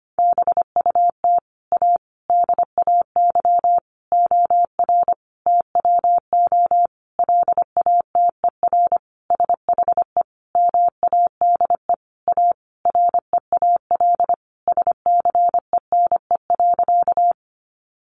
Morse code, heard as a keyed tone, about 25 wpm, 700 Hz. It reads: BUT A DAY OR TWO LATER SHE MADE A REAL SCENE.
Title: Sentences from Top 1100 Words 25wpm